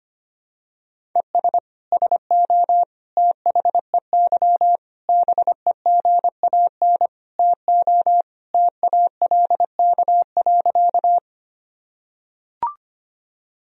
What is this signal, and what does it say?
Morse code, heard as a keyed tone, about 25 wpm, 700 Hz. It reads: ES SO THEY BEGAN TO TALK.
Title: Sentences from Top 400 Words and Abbreviations 25wpm